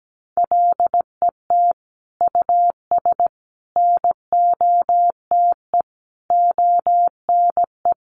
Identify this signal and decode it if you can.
Morse code, heard as a keyed tone, about 17 wpm, 700 Hz. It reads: LET US NOTE ONE
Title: Sentences from Top 1500 Words 17wpm